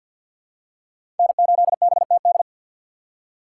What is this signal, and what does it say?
Morse code, heard as a keyed tone, about 51 wpm, 700 Hz. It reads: N8BTD
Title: USA Call Signs - 5 Characters - 3x Character Spacing - 50wpm